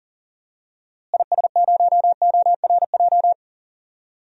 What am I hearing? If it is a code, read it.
Morse code, heard as a keyed tone, about 39 wpm, 700 Hz. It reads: IS0ORJ
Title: DXCC-100 Call Signs - 6 Characters - Repeated 2x - 40wpm